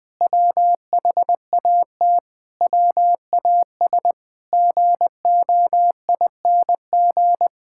Morse code, heard as a keyed tone, about 20 wpm, 700 Hz. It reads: WHAT WAS GOING
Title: Sentences with 8-Word Phrases - 20wpm